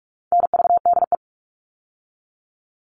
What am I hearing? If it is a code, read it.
Morse code, heard as a keyed tone, about 44 wpm, 700 Hz. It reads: NVDE
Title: All Cut Numbers - 4-Digit 45wpm